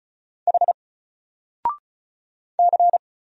Morse code, heard as a keyed tone, about 35 wpm, 700 Hz. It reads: H C
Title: Single Character (Letter:Number:Punctuation:Prosign) 35wpm